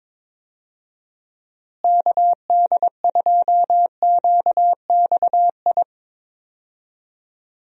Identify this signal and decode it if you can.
Morse code, heard as a keyed tone, about 22 wpm, 700 Hz. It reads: KD2QXI
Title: USA Call Signs - 6 Characters - Repeated 3x - 22wpm